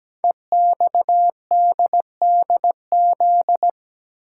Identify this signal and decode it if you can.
Morse code, heard as a keyed tone, about 17 wpm, 700 Hz. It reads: EXDDZ
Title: Code Groups - 5 Letters - 17wpm